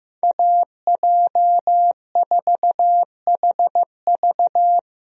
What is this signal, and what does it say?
Morse code, heard as a keyed tone, about 15 wpm, 700 Hz. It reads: AJ4HV
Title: USA Call Signs - 5 Characters - Repeated 3x - 15wpm